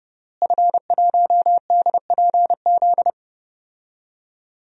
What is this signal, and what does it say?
Morse code, heard as a keyed tone, about 32 wpm, 700 Hz. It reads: F1DPZ
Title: DXCC-100 Call Signs - 5 Characters - Repeated 3x - 30wpm